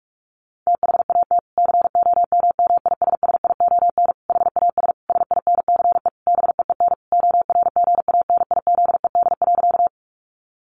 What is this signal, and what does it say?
Morse code, heard as a keyed tone, about 44 wpm, 700 Hz. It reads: THAT COMMISSION HAS SINCE BEEN ORGANIZED.
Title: Sentences from Top 1400 Words 45wpm